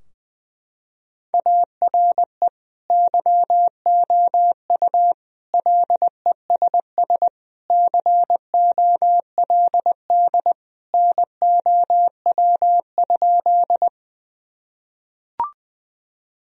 Morse code, heard as a keyed tone, about 20 wpm, 700 Hz. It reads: ARE YOU LESS COLD NOW?
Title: Sentences from Top 1000 Words - Review Part 2 - 20wpm